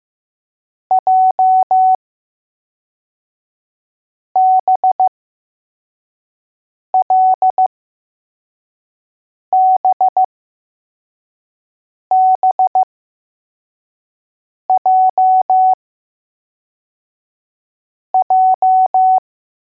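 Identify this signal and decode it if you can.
Morse code, heard as a keyed tone, about 15 wpm, spaced as for 4 wpm, 750 Hz. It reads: JBLBBJJ
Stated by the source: Morse code letters test drill for L, J and B.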